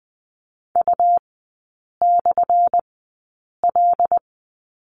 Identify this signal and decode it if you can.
Morse code, heard as a keyed tone, about 20 wpm, spaced as for 8 wpm, 700 Hz. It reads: U/L
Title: 178 Lesson 29e - Slash - Three Characters 20wpm